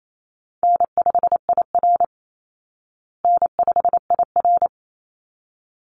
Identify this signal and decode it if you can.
Morse code, heard as a keyed tone, about 28 wpm, 700 Hz. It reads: N5IR N5IR
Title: Easy Call Signs Repeated 2x - 28wpm